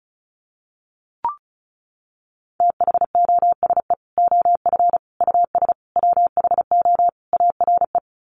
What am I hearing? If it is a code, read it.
Morse code, heard as a keyed tone, about 34 wpm, 700 Hz. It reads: THOSE OF US WHO ARE
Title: Sets of 5 Words Encore2 35wpm